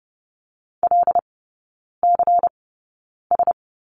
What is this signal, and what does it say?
Morse code, heard as a keyed tone, about 30 wpm, spaced as for 9 wpm, 700 Hz. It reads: LCS
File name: Lesson 16e - Letter c - Three Characters 30wpm